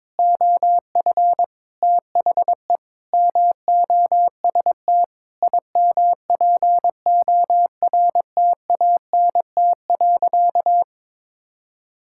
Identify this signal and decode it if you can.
Morse code, heard as a keyed tone, about 22 wpm, 700 Hz. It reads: OF THE MOST IMPORTANT.
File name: Sentences with 9-Word Phrases - 22wpm